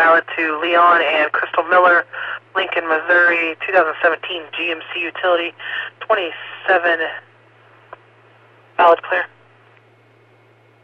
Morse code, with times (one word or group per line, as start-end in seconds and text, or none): none